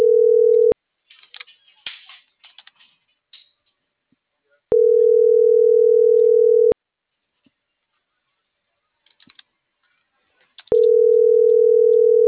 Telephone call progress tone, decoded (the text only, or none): ringback tone